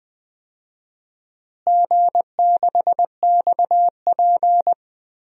Morse code, heard as a keyed tone, about 20 wpm, 700 Hz. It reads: G6XP
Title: DXCC-100 Call Signs - 4 Characters - Repeated 3x - 20wpm